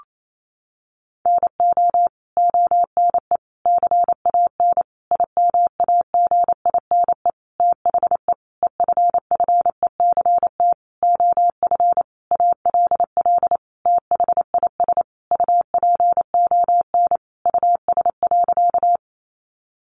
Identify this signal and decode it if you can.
Morse code, heard as a keyed tone, about 28 wpm, 700 Hz. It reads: NO ONE CAN IMAGINE THE EFFECT OF ALL THIS UPON US.
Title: Sentences from Top 1400 Words 28wpm